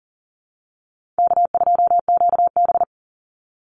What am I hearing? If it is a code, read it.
Morse code, heard as a keyed tone, about 40 wpm, 700 Hz. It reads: K2QB
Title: Call Signs Repeated 3x - 40wpm